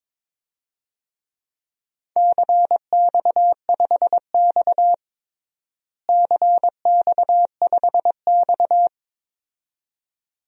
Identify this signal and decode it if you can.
Morse code, heard as a keyed tone, about 22 wpm, 700 Hz. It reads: CX5X CX5X
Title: DXCC-100 Call Signs - 4 Characters - Repeated 2x - 22wpm